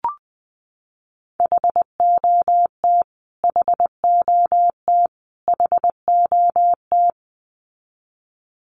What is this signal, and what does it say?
Morse code, heard as a keyed tone, about 19 wpm, 700 Hz. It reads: HOT HOT HOT
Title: Top 500 Words Repeated 20wpm